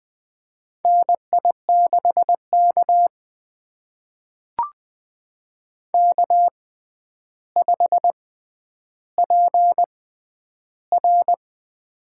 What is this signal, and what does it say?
Morse code, heard as a keyed tone, about 20 wpm, spaced as for 10 wpm, 700 Hz. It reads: NI6K K5PR
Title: USA Call Signs - 4 Characters - 6x Character Spacing - 20wpm